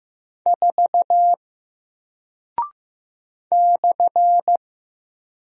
Morse code, heard as a keyed tone, about 15 wpm, 700 Hz. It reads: 4 /
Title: Single Character (Letter:Number:Punctuation:Prosign) Alt 15wpm